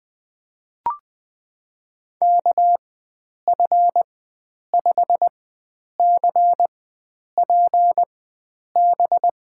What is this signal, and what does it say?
Morse code, heard as a keyed tone, about 21 wpm, spaced as for 9 wpm, 700 Hz. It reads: KF5CPB
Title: USA Call Signs - 6 Characters - 4x Character Spacing - 20wpm